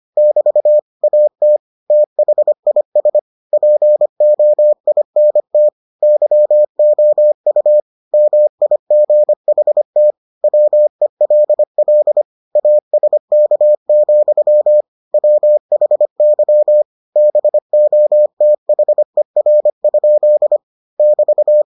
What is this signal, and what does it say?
Morse code, heard as a keyed tone, about 25 wpm, 600 Hz. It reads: <BT> AT THIS POINT YOU MIGHT WELL ASK, WHY BOTHER? <BT>